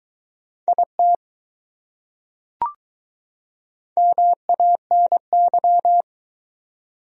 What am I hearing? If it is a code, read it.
Morse code, heard as a keyed tone, about 21 wpm, 700 Hz. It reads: IT MANY